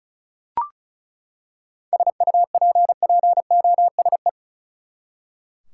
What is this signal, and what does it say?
Morse code, heard as a keyed tone, about 34 wpm, 700 Hz. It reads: SUPPOSE
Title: Seven Letter Words - 35wpm